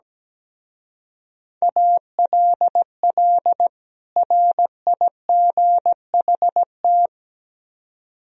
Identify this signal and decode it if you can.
Morse code, heard as a keyed tone, about 17 wpm, 700 Hz. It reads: ALL RIGHT
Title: Speed-Racing - Sets of 2 Words 17wpm